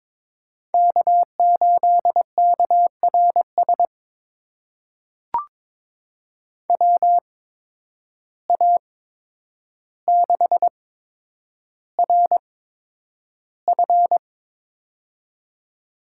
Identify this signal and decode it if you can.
Morse code, heard as a keyed tone, about 22 wpm, spaced as for 9 wpm, 700 Hz. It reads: K8KRS WA6RF
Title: USA Call Signs - 5 Characters - 8x Character Spacing - 22wpm